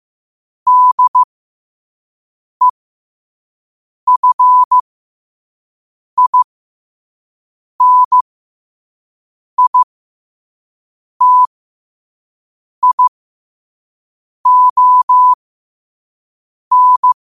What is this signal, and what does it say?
Morse code, heard as a keyed tone, about 15 wpm, spaced as for 5 wpm, 1000 Hz. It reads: DEFINITION
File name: 5wpm